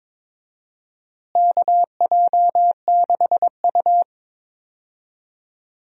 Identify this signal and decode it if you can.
Morse code, heard as a keyed tone, about 22 wpm, 700 Hz. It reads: KJ6U
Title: USA Call Signs - 4 Characters - Repeated 3x - 22wpm